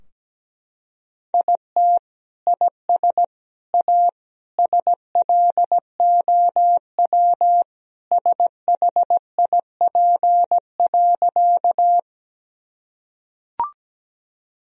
Morse code, heard as a keyed tone, about 17 wpm, 700 Hz. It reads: IT IS A SLOW SHIP.
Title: Sentences from Top 1000 Words - Review Part 2 - 17wpm